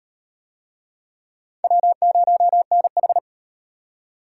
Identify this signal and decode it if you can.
Morse code, heard as a keyed tone, about 37 wpm, 700 Hz. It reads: W0NH